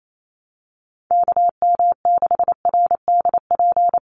Morse code, heard as a keyed tone, about 28 wpm, 700 Hz. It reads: KM6RDP
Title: USA Call Signs 28wpm